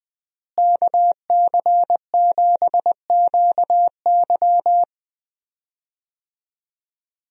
Morse code, heard as a keyed tone, about 20 wpm, 700 Hz. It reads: KC7QY
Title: USA Call Signs - 5 Characters - Repeated 2x - 20wpm